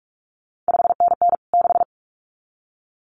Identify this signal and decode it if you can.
Morse code, heard as a keyed tone, about 46 wpm, 700 Hz. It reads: 5NN 6
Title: CQ WW DX Contest Exchange Encore - 45wpm